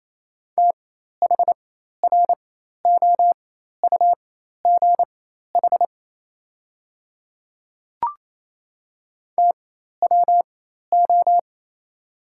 Morse code, heard as a keyed tone, about 27 wpm, spaced as for 13 wpm, 690 Hz. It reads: THROUGH TWO
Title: 4x Character Spacing - Top 200 Words - 28wpm